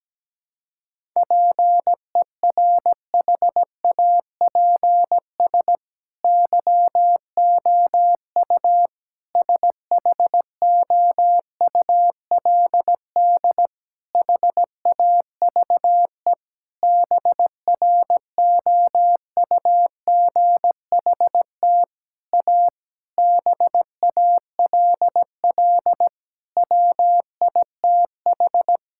Morse code, heard as a keyed tone, about 17 wpm, 700 Hz. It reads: PERHAPS YOU SHOULD HAVE BROUGHT A BALL WITH
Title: Sentences from Top 1000 Words - Review Part 6 - 17wpm